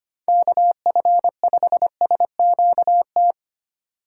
Morse code, heard as a keyed tone, about 25 wpm, 700 Hz. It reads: KF5SQT
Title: USA Call Signs - 6 Characters - 4x Character Spacing - 25wpm